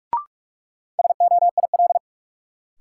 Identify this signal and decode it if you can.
Morse code, heard as a keyed tone, about 41 wpm, 700 Hz. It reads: SOIL
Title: Sentences from Top 1000 Words - Review Part 4 - New Word Drill - 45wpm